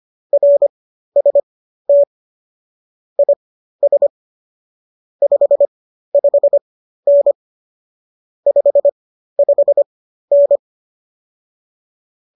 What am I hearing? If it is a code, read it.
Morse code, heard as a keyed tone, about 25 wpm, spaced as for 13 wpm, 570 Hz. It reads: RST IS 55N 55N